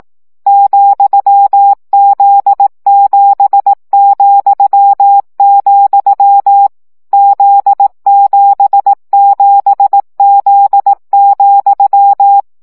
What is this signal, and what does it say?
Morse code, heard as a keyed tone, about 18 wpm, 800 Hz. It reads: ,Z7,, Z77Z,
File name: CW gyakorlas z 7 vesszo 15 baud